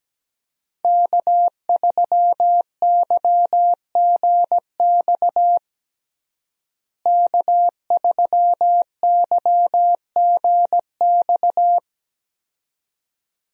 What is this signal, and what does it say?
Morse code, heard as a keyed tone, about 17 wpm, 700 Hz. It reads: K3YGX K3YGX
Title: USA Call Signs - 5 Characters - Repeated 2x - 17wpm